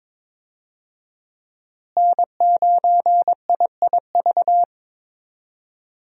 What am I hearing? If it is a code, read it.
Morse code, heard as a keyed tone, about 21 wpm, 700 Hz. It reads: N9IIV
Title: USA Call Signs - 5 Characters - Repeated 2x - 22wpm